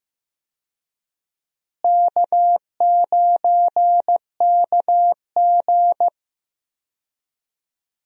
Morse code, heard as a keyed tone, about 15 wpm, 700 Hz. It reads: K9KG
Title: USA Call Signs - 4 Characters - Repeated 3x - 15wpm